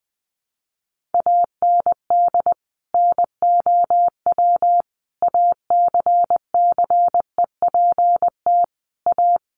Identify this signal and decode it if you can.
Morse code, heard as a keyed tone, about 20 wpm, 700 Hz. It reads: AND NOW ACCEPT A
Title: Sentences from Top 1300 Words 20wpm